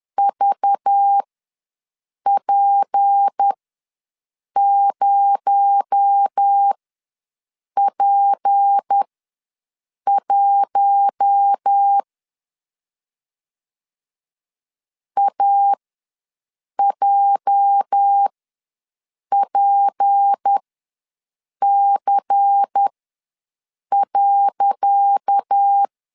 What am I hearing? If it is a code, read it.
Morse code, heard as a keyed tone, about 10 wpm, spaced as for 6 wpm, 800 Hz. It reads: VP0P1 AJPC.
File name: LEZIONI / 10 SIGNS / 10C DOT ws 10wpm 4213